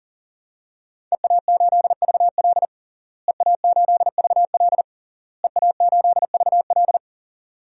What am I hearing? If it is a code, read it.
Morse code, heard as a keyed tone, about 40 wpm, 700 Hz. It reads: EA8VL EA8VL EA8VL
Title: DXCC-100 Call Signs - 5 Characters - Repeated 3x - 40wpm